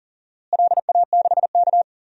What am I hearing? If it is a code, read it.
Morse code, heard as a keyed tone, about 40 wpm, 700 Hz. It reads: LA6K
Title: DXCC-100 Call Signs - 4 Characters - Repeated 3x - 40wpm